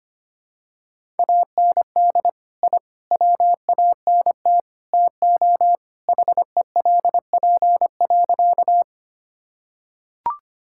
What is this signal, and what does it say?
Morse code, heard as a keyed tone, about 25 wpm, 700 Hz. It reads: AND I WANT TO HELP.
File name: practice / 25wpm / Sentences with 4-Word Phrases - 25wpm